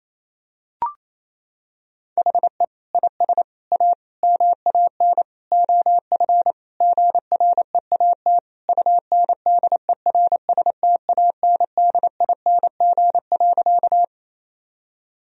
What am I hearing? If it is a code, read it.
Morse code, heard as a keyed tone, about 28 wpm, 700 Hz. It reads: HE IS A MAN OF GREAT UNDERSTANDING.
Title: Sentences from Top 1000 Words - Review Part 3 - 28wpm